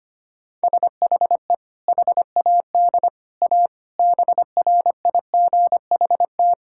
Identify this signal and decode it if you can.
Morse code, heard as a keyed tone, about 25 wpm, 700 Hz. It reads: SHE HAD A BRIGHT
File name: practice / 25wpm / Sentences with 2-Word Phrases - 25wpm